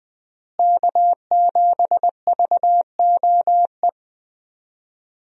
Morse code, heard as a keyed tone, about 20 wpm, 700 Hz. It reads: K7VOE
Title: USA Call Signs - 5 Characters - Repeated 2x - 20wpm